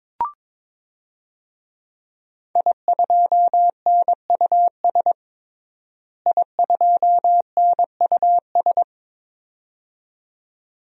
Morse code, heard as a keyed tone, about 22 wpm, 700 Hz. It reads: I2NUS I2NUS
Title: DXCC-100 Call Signs - 5 Characters - Repeated 2x - 22wpm